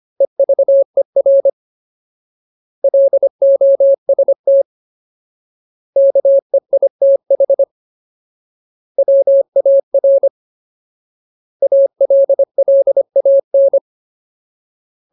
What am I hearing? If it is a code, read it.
Morse code, heard as a keyed tone, about 25 wpm, 550 Hz. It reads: EVER LOST KEITH WAR ALLAN
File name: Ultimate Word List / 25WPM Ultimate